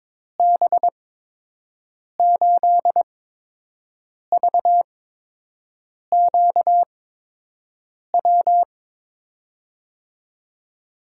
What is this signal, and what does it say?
Morse code, heard as a keyed tone, about 22 wpm, spaced as for 6 wpm, 700 Hz. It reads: B8VQW
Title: USA Call Signs - 6 Characters - 8x Character Spacing - 22wpm